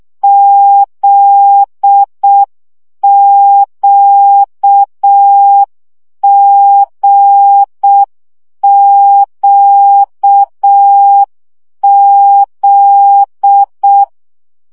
Morse code, heard as a keyed tone, about 6 wpm, 800 Hz. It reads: ZQGQZ
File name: CW gyakorlas g z q 5 baud